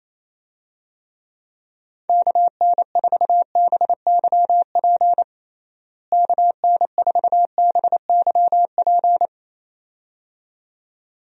Morse code, heard as a keyed tone, about 28 wpm, 700 Hz. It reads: KN4BYP KN4BYP
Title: USA Call Signs - 6 Characters - Repeated 2x - 28wpm